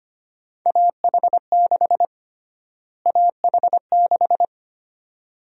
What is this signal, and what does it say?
Morse code, heard as a keyed tone, about 25 wpm, 700 Hz. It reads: AH6 AH6
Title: USA Call Sign Prefixes - Repeated 2x - 25wpm